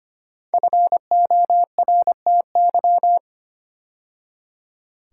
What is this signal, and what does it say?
Morse code, heard as a keyed tone, about 25 wpm, 700 Hz. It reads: FORTY
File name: 5 Letter Words - First 3 Letters Given - 25wpm